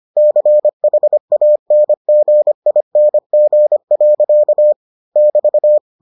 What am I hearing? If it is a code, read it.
Morse code, heard as a keyed tone, about 25 wpm, 600 Hz. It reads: CHANGING. <BT>